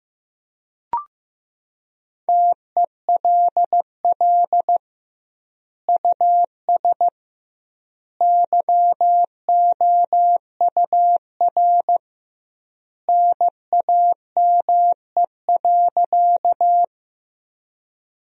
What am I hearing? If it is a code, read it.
Morse code, heard as a keyed tone, about 16 wpm, 700 Hz. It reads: TELL US YOUR NAME.
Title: 2x Word-Spacing - Sentences from Top 200 Words - 15wpm